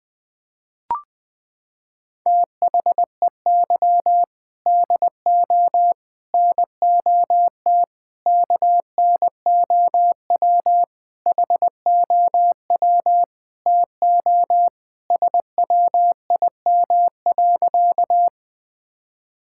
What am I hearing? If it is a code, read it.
Morse code, heard as a keyed tone, about 20 wpm, 700 Hz. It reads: THEY DO NOT KNOW HOW TO SWIM.
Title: Sentences with 5-Word Phrases - 20wpm